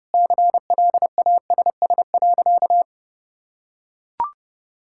Morse code, heard as a keyed tone, about 30 wpm, 700 Hz. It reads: CLASS.
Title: Sentences from Top 1000 Words - Review Part 6 - 30wpm